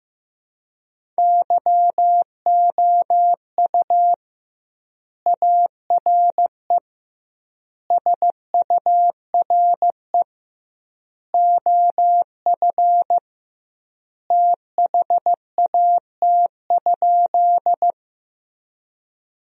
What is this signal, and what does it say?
Morse code, heard as a keyed tone, about 15 wpm, 700 Hz. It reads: YOU ARE SURE OF THAT?
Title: 2x Word-Spacing - Sentences from Top 300 Words - 15wpm